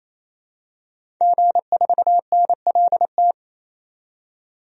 Morse code, heard as a keyed tone, about 28 wpm, 700 Hz. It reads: G4NLT
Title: DXCC-100 Call Signs - 5 Characters - 28wpm